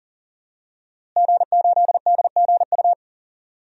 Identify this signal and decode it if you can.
Morse code, heard as a keyed tone, about 39 wpm, 700 Hz. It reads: G8DGU
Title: DXCC-100 Call Signs - 5 Characters - Repeated 3x - 40wpm